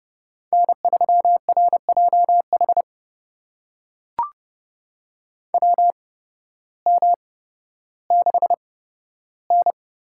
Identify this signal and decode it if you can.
Morse code, heard as a keyed tone, about 30 wpm, spaced as for 14 wpm, 700 Hz. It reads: N3RJH WM6N
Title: USA Call Signs - 8x Character Spacing - 30wpm